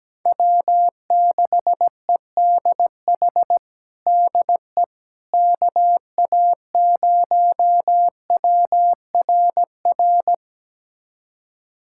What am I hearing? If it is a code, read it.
Morse code, heard as a keyed tone, about 17 wpm, 700 Hz. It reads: W6EDH DE KA0WRR
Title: USA Callsign DE Callsign 17wpm